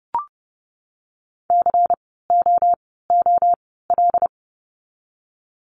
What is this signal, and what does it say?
Morse code, heard as a keyed tone, about 30 wpm, spaced as for 17 wpm, 700 Hz. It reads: COOL
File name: Lesson 16f - Letter c - Words 30wpm